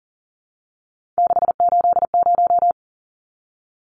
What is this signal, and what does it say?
Morse code, heard as a keyed tone, about 40 wpm, 700 Hz. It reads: 680
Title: Numbers - 3-Digit 40wpm